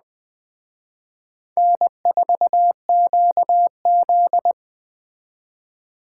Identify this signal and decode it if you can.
Morse code, heard as a keyed tone, about 19 wpm, 700 Hz. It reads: N4QZ